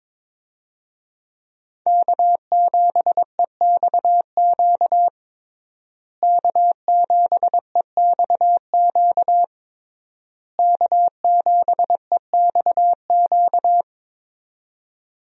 Morse code, heard as a keyed tone, about 22 wpm, 700 Hz. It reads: K7EXQ K7EXQ K7EXQ
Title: USA Call Signs - 5 Characters - Repeated 3x - 22wpm